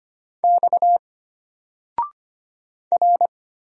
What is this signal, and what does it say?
Morse code, heard as a keyed tone, about 25 wpm, 700 Hz. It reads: X R
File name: Single-Letter-Number-25wpm